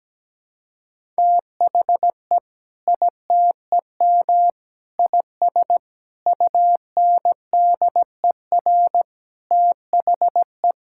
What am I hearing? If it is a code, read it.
Morse code, heard as a keyed tone, about 17 wpm, 700 Hz. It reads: THE ITEM IS UNDER THE
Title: Sentences from Top 1000 Words - Review Part 6 - 17wpm